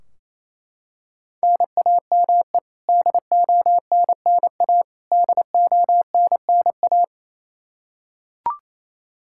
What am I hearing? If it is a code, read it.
Morse code, heard as a keyed tone, about 28 wpm, 700 Hz. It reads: NAME DONNA DONNA
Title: Practice QSO - Exchange 1 - Beginning - 28wpm